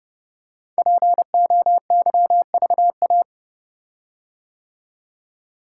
Morse code, heard as a keyed tone, about 30 wpm, 700 Hz. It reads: POYVA
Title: Code Groups - 5 Letters - 30wpm